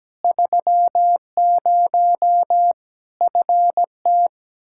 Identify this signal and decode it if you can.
Morse code, heard as a keyed tone, about 17 wpm, 700 Hz. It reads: 30 FT
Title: Practice QSO - Exchange 2 - Info - 17wpm